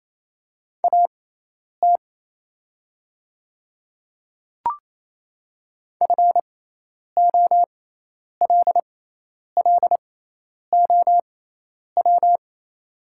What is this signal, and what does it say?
Morse code, heard as a keyed tone, about 27 wpm, spaced as for 10 wpm, 700 Hz. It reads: AT FOLLOW
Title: 6x Character Spacing - Top 200 Words - 28wpm